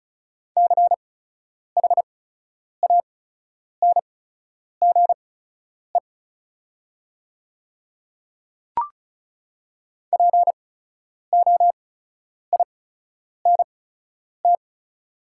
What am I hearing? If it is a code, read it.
Morse code, heard as a keyed tone, about 36 wpm, spaced as for 10 wpm, 700 Hz. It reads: CHANGE POINT
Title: 8x Character Spacing - Top 200 Words - 35wpm